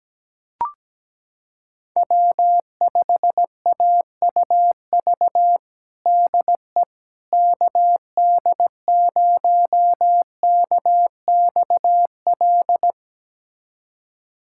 Morse code, heard as a keyed tone, about 17 wpm, 700 Hz. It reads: W5AUV DE KD0KXL
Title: USA Callsign DE Callsign 17wpm